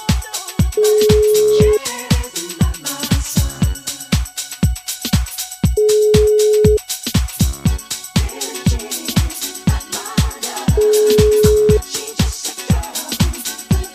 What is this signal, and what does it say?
Ringback tone (Europe, 425 Hz).